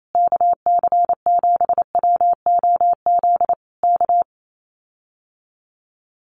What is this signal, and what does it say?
Morse code, heard as a keyed tone, about 28 wpm, 700 Hz. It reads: KC7WOZ K
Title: Common QSO Phrases Encore 28wpm